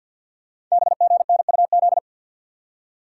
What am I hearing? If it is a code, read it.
Morse code, heard as a keyed tone, about 48 wpm, 700 Hz. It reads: BGNUZ
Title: Code Groups - 5 Letters - 50wpm